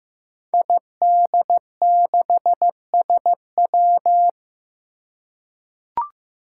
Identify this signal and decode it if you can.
Morse code, heard as a keyed tone, about 15 wpm, 700 Hz. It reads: ID6SW
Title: Five Challenging Letter-Number Combinations 15wpm